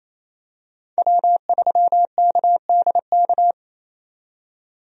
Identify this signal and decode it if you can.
Morse code, heard as a keyed tone, about 28 wpm, 700 Hz. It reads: W3KDK
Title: USA Call Signs - 5 Characters - 28wpm